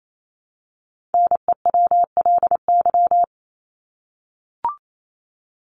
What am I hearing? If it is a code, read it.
Morse code, heard as a keyed tone, about 28 wpm, 700 Hz. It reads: NEWLY
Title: LY Suffix - Top 100 Words with Base Words 28wpm